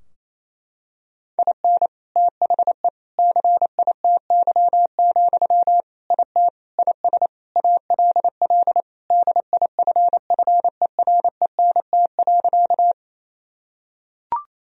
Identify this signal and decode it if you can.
Morse code, heard as a keyed tone, about 28 wpm, 700 Hz. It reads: IN THE CITY, IT IS ALL DIFFERENT.
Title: Sentences from Top 1000 Words - Review Part 3 - 28wpm